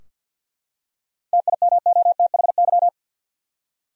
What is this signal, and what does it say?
Morse code, heard as a keyed tone, about 52 wpm, 700 Hz. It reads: TIMOTHY
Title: USA Common Names - 50wpm